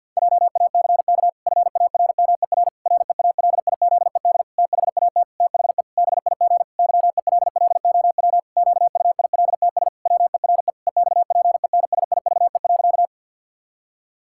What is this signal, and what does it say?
Morse code, heard as a keyed tone, about 51 wpm, 700 Hz. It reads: JACK FARMER REALIZED THAT THE BIG YELLOW QUILTS WERE EXPENSIVE.
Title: Pangram Challenge v2 - 50wpm